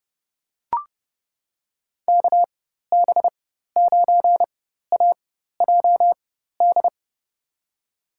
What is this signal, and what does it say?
Morse code, heard as a keyed tone, about 31 wpm, spaced as for 14 wpm, 700 Hz. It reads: KB9AJD
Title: USA Call Signs - 6 Characters - 4x Character Spacing - 30wpm